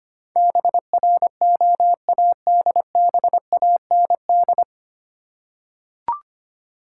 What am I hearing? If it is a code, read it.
Morse code, heard as a keyed tone, about 25 wpm, 700 Hz. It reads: BROADBAND
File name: Nine Letter Words - 25wpm